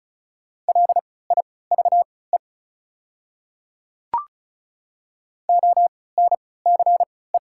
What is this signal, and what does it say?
Morse code, heard as a keyed tone, about 36 wpm, spaced as for 20 wpm, 700 Hz. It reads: LIVE ONCE
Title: 3x Character Spacing - Top 300 Words - 35wpm